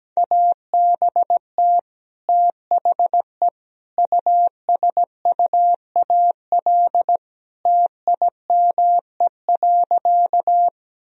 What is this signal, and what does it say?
Morse code, heard as a keyed tone, about 17 wpm, 700 Hz. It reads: ABT THE USUAL TIME.
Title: Sentences from Top 400 Words and Abbreviations 17wpm